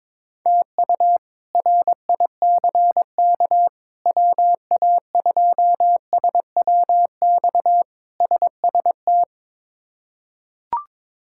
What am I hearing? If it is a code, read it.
Morse code, heard as a keyed tone, about 22 wpm, 700 Hz. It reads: TU RICK WA2SWX SST
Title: K1USN SST QSO Practice - 22wpm